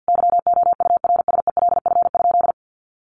Morse code, heard as a keyed tone, about 50 wpm, 700 Hz. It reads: YOURSELF?
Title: Sentences from Top 1500 Words 50wpm